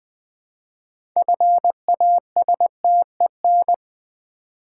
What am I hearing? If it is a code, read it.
Morse code, heard as a keyed tone, about 20 wpm, 700 Hz. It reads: FASTEN
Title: 6 Letter Words - First 3 Letters Given - 20wpm